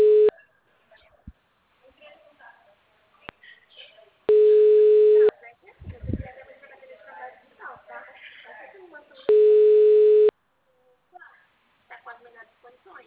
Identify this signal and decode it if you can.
Ringback tone (Europe, 425 Hz), with speech.